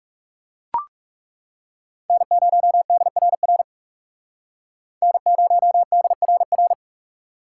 Morse code, heard as a keyed tone, about 37 wpm, 700 Hz. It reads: N0DRR N0DRR
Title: Speed Racing - USA 1x3 Call Signs 30wpm